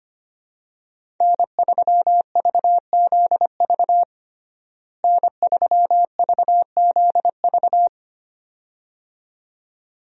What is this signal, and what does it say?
Morse code, heard as a keyed tone, about 25 wpm, 700 Hz. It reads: N3VZV N3VZV
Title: USA Call Signs - 5 Characters - Repeated 2x - 25wpm